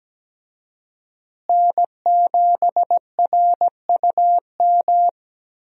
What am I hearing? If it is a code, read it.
Morse code, heard as a keyed tone, about 17 wpm, 700 Hz. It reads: N7RUM
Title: USA Call Signs - 5 Characters - Repeated 2x - 17wpm